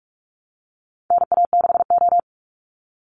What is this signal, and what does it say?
Morse code, heard as a keyed tone, about 44 wpm, 700 Hz. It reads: NA6O
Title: Call Signs Repeated 2x - 45wpm